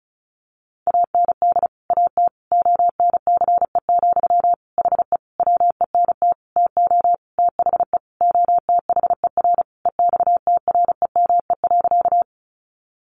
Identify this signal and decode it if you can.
Morse code, heard as a keyed tone, about 34 wpm, 700 Hz. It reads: AND AT ONCE, HE WENT TO THE OTHER EXTREME.
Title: Sentences from Top 1300 Words 35wpm